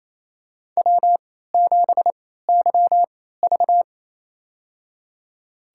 Morse code, heard as a keyed tone, about 28 wpm, spaced as for 16 wpm, 700 Hz. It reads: W7YV
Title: USA Call Signs - 4 Characters - 3x Character Spacing - 28wpm